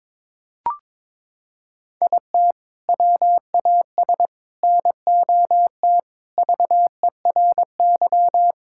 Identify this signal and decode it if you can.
Morse code, heard as a keyed tone, about 21 wpm, 700 Hz. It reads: IT WAS NOT VERY
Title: Sentences from Top 1000 Words - Review Part 5 - 22wpm